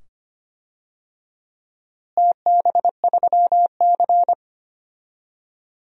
Morse code, heard as a keyed tone, about 25 wpm, 700 Hz. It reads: TB3C
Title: DXCC-100 Call Signs - 4 Characters - Repeated 2x - 25wpm